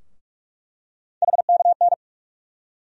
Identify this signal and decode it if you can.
Morse code, heard as a keyed tone, about 47 wpm, 700 Hz. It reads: HKN
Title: USA Call Sign Postfixes - 3 Letters 45wpm